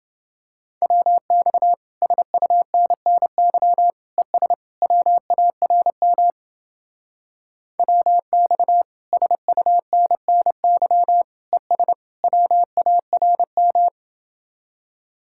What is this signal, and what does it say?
Morse code, heard as a keyed tone, about 31 wpm, 700 Hz. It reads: WX SUNNY ES WARM WX SUNNY ES WARM